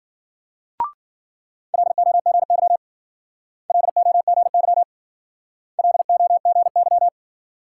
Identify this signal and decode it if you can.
Morse code, heard as a keyed tone, about 56 wpm, 700 Hz. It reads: LOGY LOGY LOGY